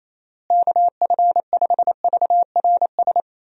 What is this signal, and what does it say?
Morse code, heard as a keyed tone, about 28 wpm, 700 Hz. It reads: KF5VRS
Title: USA Call Signs - 6 Characters - Repeated 2x - 28wpm